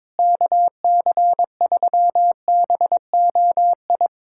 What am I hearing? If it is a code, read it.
Morse code, heard as a keyed tone, about 22 wpm, 700 Hz. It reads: KC3BOI
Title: USA Callsign DE Callsign 22wpm